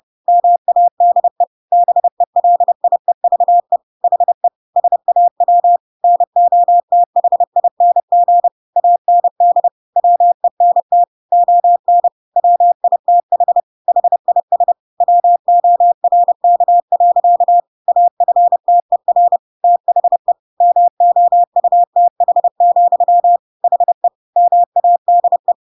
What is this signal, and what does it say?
Morse code, heard as a keyed tone, about 30 wpm, 700 Hz. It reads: MADE BELIEVE HE SAW NOTHING AND WENT ON WITH HIS WORK. AFTER THE MOUTH, HE MADE
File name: Pinocchio / Pinocchio ch3 30 30 ik1hge